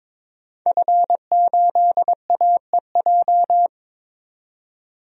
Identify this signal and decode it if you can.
Morse code, heard as a keyed tone, about 22 wpm, 700 Hz. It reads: F8AEJ
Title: DXCC-100 Call Signs - 5 Characters - 22wpm